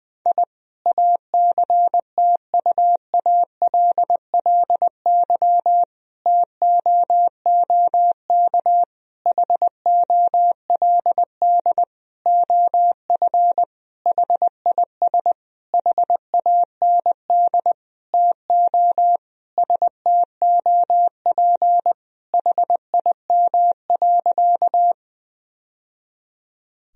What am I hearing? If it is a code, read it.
Morse code, heard as a keyed tone, about 21 wpm, 700 Hz. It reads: I ACTUALLY TOOK HOLD OF HIS HAND TO STOP HIM.
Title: Sentences from Top 1000 Words - Review Part 4 - 20wpm